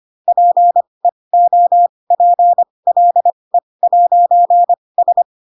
Morse code, heard as a keyed tone, about 25 wpm, spaced as for 20 wpm, 700 Hz. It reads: PEOPLE'S
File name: Pinocchio / Pinocchio ch6 15 25 ik1hge